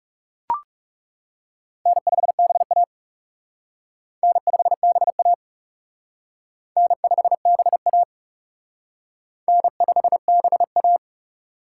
Morse code, heard as a keyed tone, about 37 wpm, 700 Hz. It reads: N5BA N5BA N5BA N5BA
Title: Speed Racing - USA 1x2 Call Signs 30wpm